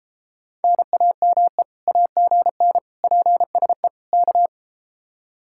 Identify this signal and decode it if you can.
Morse code, heard as a keyed tone, about 31 wpm, 700 Hz. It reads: NAME AGN PSE K